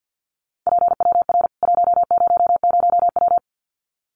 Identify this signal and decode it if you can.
Morse code, heard as a keyed tone, about 50 wpm, 700 Hz. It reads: PWR 100W
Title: Common QSO Phrases 50wpm